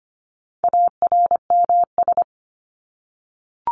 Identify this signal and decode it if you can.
Morse code, heard as a keyed tone, about 25 wpm, 700 Hz. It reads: ARMS
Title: ES:S Suffix - Top 100 Words with Base Words 25wpm